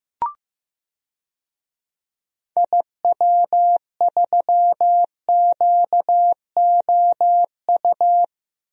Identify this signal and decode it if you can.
Morse code, heard as a keyed tone, about 15 wpm, 700 Hz. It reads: IW3QOU
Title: DXCC-100 Call Signs - 6 Characters - Repeated 3x - 15wpm